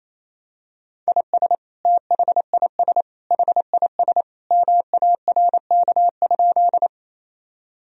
Morse code, heard as a keyed tone, about 28 wpm, 700 Hz. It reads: IS THIS HIS MARK?
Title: Sentences from Top 1000 Words - Review Part 1 - 28wpm